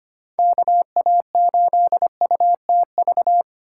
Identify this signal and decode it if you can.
Morse code, heard as a keyed tone, about 25 wpm, 700 Hz. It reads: KA8UTV
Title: USA Call Signs - 6 Characters - 2x Character Spacing - 25wpm